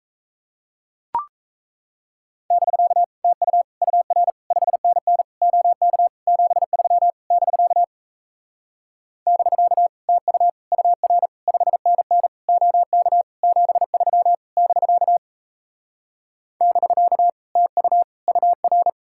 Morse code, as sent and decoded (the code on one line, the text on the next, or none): -...-.- / - ..- / ..- .-. / ..... -. -. / --- -.- / --... ...-- / -...-.- / -...-.- / - ..- / ..- .-. / ..... -. -. / --- -.- / --... ...-- / -...-.- / -...-.- / - ..- / ..- .-.
<BK> TU UR 5NN OK 73 <BK> <BK> TU UR 5NN OK 73 <BK> <BK> TU UR